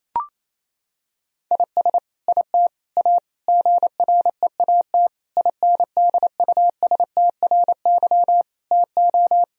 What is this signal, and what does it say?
Morse code, heard as a keyed tone, about 27 wpm, 700 Hz. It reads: IS IT A GREAT INDUSTRY TO
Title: Sentences from Top 1000 Words - Review Part 4 - 28wpm